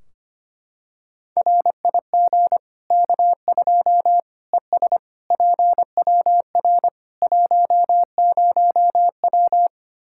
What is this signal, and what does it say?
Morse code, heard as a keyed tone, about 25 wpm, 700 Hz. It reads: RIG K2 ES PWR 10W
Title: Practice QSO - Exchange 2 - Info - 25wpm